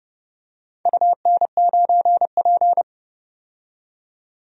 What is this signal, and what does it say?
Morse code, heard as a keyed tone, about 31 wpm, 700 Hz. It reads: UN9P
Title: DXCC-100 Call Signs - 4 Characters - 30wpm